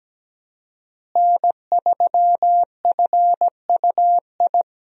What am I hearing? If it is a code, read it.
Morse code, heard as a keyed tone, about 17 wpm, 700 Hz. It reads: N3FUI
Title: USA Call Signs - 5 Characters - Repeated 2x - 17wpm